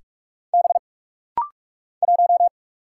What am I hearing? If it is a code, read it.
Morse code, heard as a keyed tone, about 44 wpm, 690 Hz. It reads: B 1
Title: Single Letter-Number - Rapid-Fire 45wpm